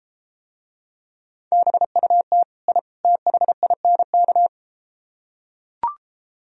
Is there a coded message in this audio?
Yes: Morse code from 1.5 to 4.5 s, heard as a keyed tone, about 33 wpm, 700 Hz. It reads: BUT I THINK